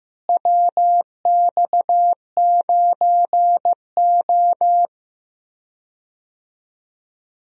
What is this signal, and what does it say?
Morse code, heard as a keyed tone, about 15 wpm, 700 Hz. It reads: WX9O
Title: USA Call Signs - 4 Characters - Repeated 2x - 15wpm